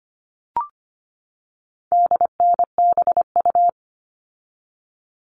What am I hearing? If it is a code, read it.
Morse code, heard as a keyed tone, about 25 wpm, 700 Hz. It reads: DNBU
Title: All Cut Numbers - 4-Digit 25wpm